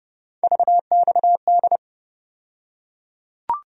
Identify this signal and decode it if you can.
Morse code, heard as a keyed tone, about 30 wpm, 700 Hz. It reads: VXD
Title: USA Call Sign Postfixes - 3 Letters 30wpm